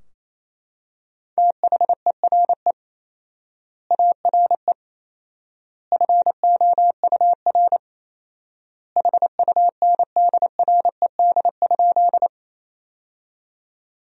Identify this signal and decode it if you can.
Morse code, heard as a keyed tone, about 28 wpm, 700 Hz. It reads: THERE ARE FOUR HUNDRED?
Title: 4x Word-Spacing - Sentences from Top 400 Words - 28wpm